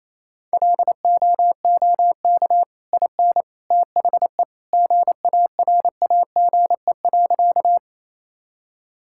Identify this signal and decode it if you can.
Morse code, heard as a keyed tone, about 28 wpm, 700 Hz. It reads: LOOK IN THE GARAGE.
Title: Sentences from Top 1600 Words 28wpm